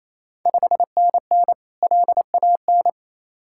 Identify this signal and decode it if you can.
Morse code, heard as a keyed tone, about 28 wpm, 700 Hz. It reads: 5NN LAN
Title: Ontario QSO Party Exchange 28wpm